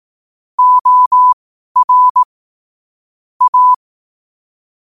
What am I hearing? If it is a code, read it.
Morse code, heard as a keyed tone, about 18 wpm, spaced as for 12 wpm, 1000 Hz. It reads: OR A
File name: qotdfiles / 12wpm